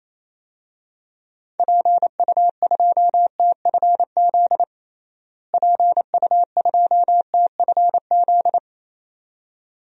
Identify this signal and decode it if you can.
Morse code, heard as a keyed tone, about 28 wpm, 700 Hz. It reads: PU2TFZ PU2TFZ
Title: DXCC-100 Call Signs - 6 Characters - Repeated 2x - 28wpm